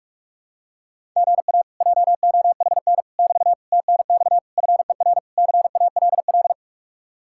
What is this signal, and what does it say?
Morse code, heard as a keyed tone, about 50 wpm, 700 Hz. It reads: GA JOHN <BT> TNX FER CALL